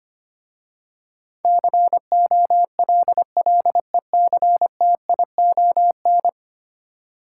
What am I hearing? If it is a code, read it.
Morse code, heard as a keyed tone, about 25 wpm, 700 Hz. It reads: COLLECTION
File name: 10 Letter Words - First 3 Letters Given - 25wpm